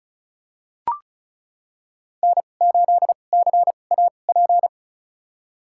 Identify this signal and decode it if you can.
Morse code, heard as a keyed tone, about 34 wpm, spaced as for 25 wpm, 700 Hz. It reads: N8CAP
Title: USA Call Signs - 2x Character Spacing - 35wpm